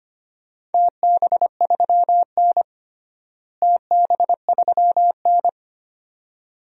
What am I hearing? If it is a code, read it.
Morse code, heard as a keyed tone, about 25 wpm, 700 Hz. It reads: TB3N TB3N
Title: DXCC-100 Call Signs - 4 Characters - Repeated 2x - 25wpm